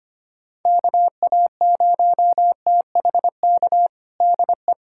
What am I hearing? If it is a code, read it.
Morse code, heard as a keyed tone, about 25 wpm, 700 Hz. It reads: KA0THK DE
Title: USA Callsign DE Callsign 25wpm